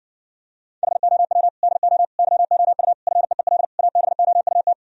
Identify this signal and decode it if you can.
Morse code, heard as a keyed tone, about 70 wpm, 700 Hz. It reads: HOW DO YOU FEEL ABOUT